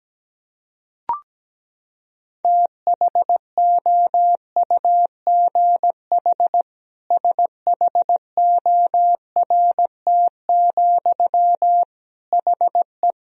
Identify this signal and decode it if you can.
Morse code, heard as a keyed tone, about 17 wpm, 700 Hz. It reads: THOUGH SHORT, HE
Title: Sentences from Top 1000 Words - Review Part 3 - 17wpm